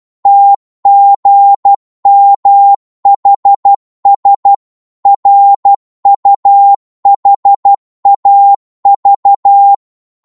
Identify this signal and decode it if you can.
Morse code, heard as a keyed tone, about 12 wpm, 800 Hz. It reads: TGMHS RUHAV
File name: Morse / ltr-12wpm